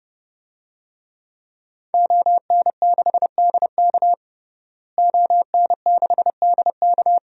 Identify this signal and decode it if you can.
Morse code, heard as a keyed tone, about 30 wpm, 700 Hz. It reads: ON6DK ON6DK
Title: DXCC-100 Call Signs - 5 Characters - Repeated 3x - 30wpm